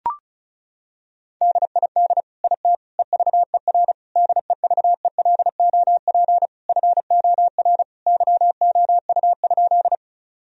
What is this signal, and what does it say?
Morse code, heard as a keyed tone, about 36 wpm, 700 Hz. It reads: DID IT EVER DEVELOP FOR YOU?
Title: Sentences from Top 1000 Words - Review Part 2 - 35wpm